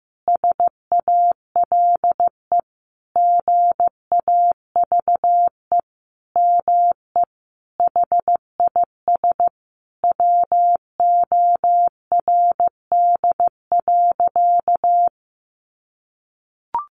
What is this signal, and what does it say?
Morse code, heard as a keyed tone, about 15 wpm, 700 Hz. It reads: SALE GAVE ME HIS WORD.
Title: Sentences from Top 1400 Words 15wpm